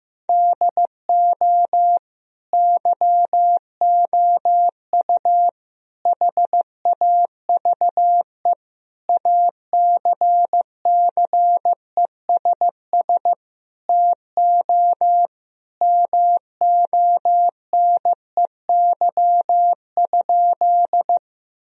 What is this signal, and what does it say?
Morse code, heard as a keyed tone, about 16 wpm, 700 Hz. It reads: DO YOU HAVE ACCESS TO MONEY?
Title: Sentences from Top 1000 Words - Review Part 4 - 15wpm